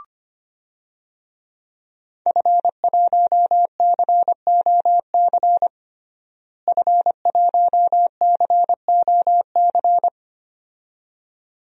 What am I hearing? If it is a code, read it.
Morse code, heard as a keyed tone, about 25 wpm, 700 Hz. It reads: F1COC F1COC
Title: DXCC-100 Call Signs - 5 Characters - Repeated 2x - 25wpm